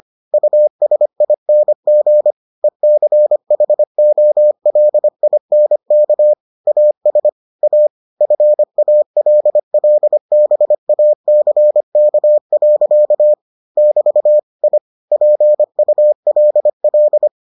Morse code, heard as a keyed tone, about 25 wpm, 600 Hz. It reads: USING ECHOLINK AS A FALLBACK. <BT> I PULL